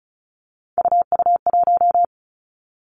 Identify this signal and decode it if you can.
Morse code, heard as a keyed tone, about 36 wpm, 700 Hz. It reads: UU1
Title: Three Challenging Letter-Number Combinations 35wpm